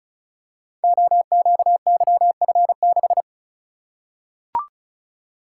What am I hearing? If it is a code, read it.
Morse code, heard as a keyed tone, about 35 wpm, 700 Hz. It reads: OQYF6
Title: Code Groups - 5 Letter:Numbers - 35wpm